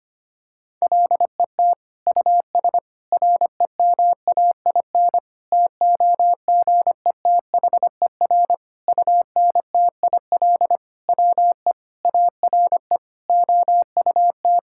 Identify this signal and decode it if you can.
Morse code, heard as a keyed tone, about 25 wpm, 700 Hz. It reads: LET US REMAIN TOGETHER UNTIL WE ARE OUT
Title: Sentences from Top 1600 Words 25wpm